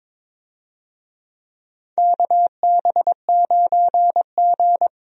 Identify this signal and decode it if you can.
Morse code, heard as a keyed tone, about 22 wpm, 700 Hz. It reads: KB9G
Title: USA Call Signs - 4 Characters - Repeated 2x - 22wpm